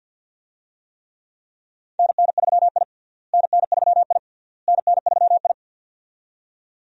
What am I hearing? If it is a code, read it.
Morse code, heard as a keyed tone, about 52 wpm, 700 Hz. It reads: NN3I NN3I NN3I
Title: USA Call Signs - 4 Characters - Repeated 3x - 50wpm